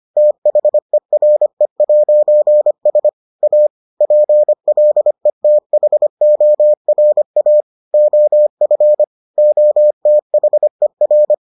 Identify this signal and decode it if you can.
Morse code, heard as a keyed tone, about 25 wpm, 600 Hz. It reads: THERE'S A PLETHORA OF OTHER